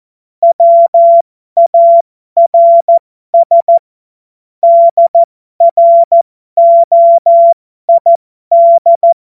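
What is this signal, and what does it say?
Morse code, heard as a keyed tone, about 14 wpm, spaced as for 12 wpm, 680 Hz. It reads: WARS DROID
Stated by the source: Morse joke practice at effective 60, Farnsworth 70.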